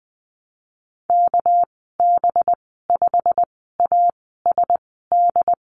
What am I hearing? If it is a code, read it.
Morse code, heard as a keyed tone, about 20 wpm, spaced as for 14 wpm, 700 Hz. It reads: KB5ASD
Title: 213 Lesson 34g - Number 8 - Call Signs 20wpm